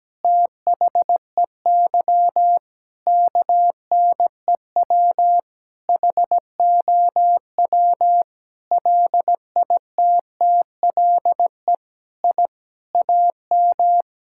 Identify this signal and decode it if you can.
Morse code, heard as a keyed tone, about 17 wpm, 700 Hz. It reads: THEY KNEW HOW LITTLE I AM
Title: Sentences from Top 1000 Words - Review Part 6 - 17wpm